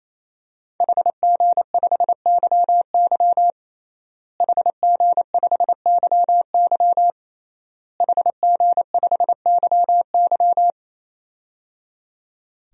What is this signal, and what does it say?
Morse code, heard as a keyed tone, about 28 wpm, 700 Hz. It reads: HG5YY HG5YY HG5YY
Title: DXCC-100 Call Signs - 5 Characters - Repeated 3x - 28wpm